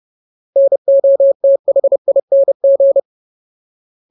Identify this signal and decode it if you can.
Morse code, heard as a keyed tone, about 30 wpm, 550 Hz. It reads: NOTHING
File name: Ultimate Word List / 30WPM Ultimate